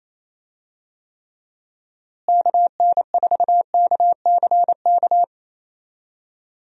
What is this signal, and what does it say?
Morse code, heard as a keyed tone, about 28 wpm, 700 Hz. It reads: KN4KCK
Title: USA Call Signs - 6 Characters - Repeated 2x - 28wpm